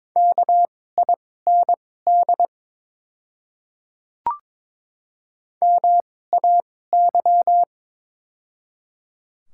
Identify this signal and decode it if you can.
Morse code, heard as a keyed tone, about 22 wpm, spaced as for 16 wpm, 700 Hz. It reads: KIND MAY
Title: 2x Character Spacing - Top 200 Words - 22wpm